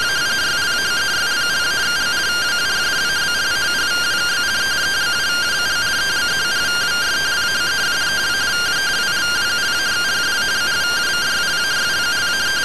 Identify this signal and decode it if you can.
Two-tone teleprinter signal, about 170 Hz shift.